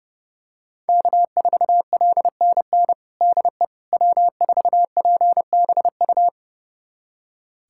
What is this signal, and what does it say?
Morse code, heard as a keyed tone, about 30 wpm, 700 Hz. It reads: K4LNN DE W4PBU
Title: USA Callsign DE Callsign - 5 Characters - 30wpm